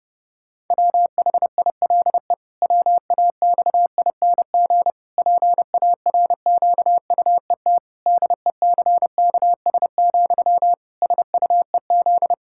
Morse code, heard as a keyed tone, about 30 wpm, 700 Hz. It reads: WHILE WAXING PARQUET DECKS, SUEZ
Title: Pangram Challenge v2 - 30wpm